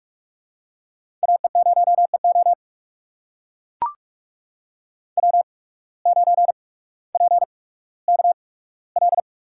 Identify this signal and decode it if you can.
Morse code, heard as a keyed tone, about 41 wpm, spaced as for 19 wpm, 700 Hz. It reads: AE0EO W9PKL
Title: USA Call Signs - 5 Characters - 8x Character Spacing - 45wpm